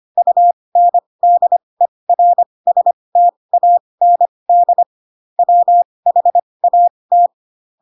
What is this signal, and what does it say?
Morse code, heard as a keyed tone, about 25 wpm, spaced as for 20 wpm, 700 Hz. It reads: UNDERSTAND WHAT
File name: Pinocchio / Pinocchio ch8 15 25 ik1hge